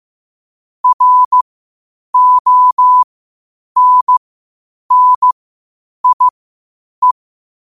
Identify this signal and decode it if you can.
Morse code, heard as a keyed tone, about 15 wpm, spaced as for 8 wpm, 1000 Hz. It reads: RONNIE
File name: qotdfiles / 8wpm